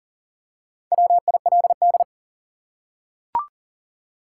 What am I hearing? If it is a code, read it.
Morse code, heard as a keyed tone, about 40 wpm, 700 Hz. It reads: WILD
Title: Four Letter Words - 40wpm